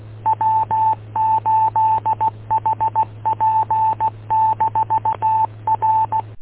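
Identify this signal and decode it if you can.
Morse code, heard as a keyed tone, about 16 wpm, 880 Hz, with radio noise. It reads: W8HP-R
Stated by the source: Morse code recorded from W8UM audio archive, 145.230.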